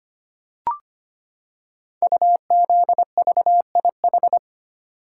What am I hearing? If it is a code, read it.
Morse code, heard as a keyed tone, about 25 wpm, 690 Hz. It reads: UZVIH
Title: Code Groups - 5 Letters - 25wpm